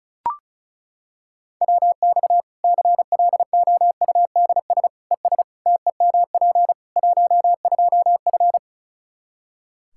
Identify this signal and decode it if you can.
Morse code, heard as a keyed tone, about 36 wpm, 700 Hz. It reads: WX CLOUDS ES TEMP 12F
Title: K1USN SST QSO Practice 35wpm